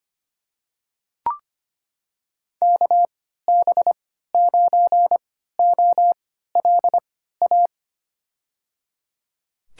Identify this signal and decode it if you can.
Morse code, heard as a keyed tone, about 25 wpm, spaced as for 14 wpm, 700 Hz. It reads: KB9OLA
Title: USA Call Signs - 3x Character Spacing - 25wpm